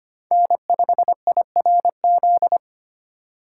Morse code, heard as a keyed tone, about 25 wpm, 700 Hz. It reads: N5IRZ
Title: USA Call Signs - 5 Characters - 2x Character Spacing - 25wpm